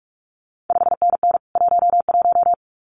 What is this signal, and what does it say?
Morse code, heard as a keyed tone, about 46 wpm, 700 Hz. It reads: 5NN 11
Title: CQ WW DX Contest Exchange Encore - 45wpm